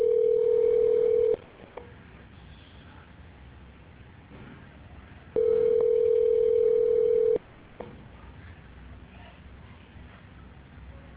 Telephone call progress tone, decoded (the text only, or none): ringback tone